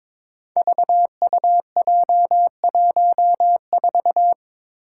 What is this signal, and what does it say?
Morse code, heard as a keyed tone, about 22 wpm, 700 Hz. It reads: VUJ14
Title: Five Challenging Letter-Number Combinations 22wpm